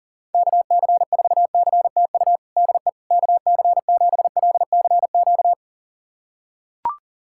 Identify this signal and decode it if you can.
Morse code, heard as a keyed tone, about 40 wpm, 690 Hz. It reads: KC4CTU DE KC7LCQ
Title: USA Callsign DE Callsign - 6 Characters - 40wpm